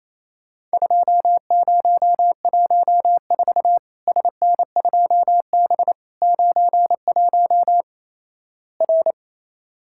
Morse code, heard as a keyed tone, about 28 wpm, 700 Hz. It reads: 2014 SN2B 91 R
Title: Worked All Europe DX Contest - QTC Exchange - 28wpm